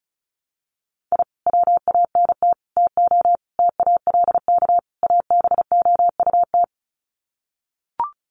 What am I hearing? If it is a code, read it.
Morse code, heard as a keyed tone, about 34 wpm, 700 Hz. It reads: I WANT TO TALK ABOUT
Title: Sets of 5 Words Encore 35wpm